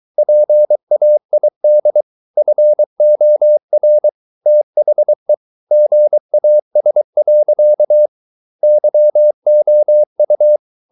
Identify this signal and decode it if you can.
Morse code, heard as a keyed tone, about 23 wpm, 600 Hz. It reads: PAID FOR THE GAS. YOU